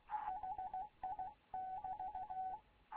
Morse code, heard as a keyed tone, about 16 wpm, 750 Hz, with radio noise. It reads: HI <BT>
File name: Morze-HA2MN / hangp / 2 11 GYAKORLAT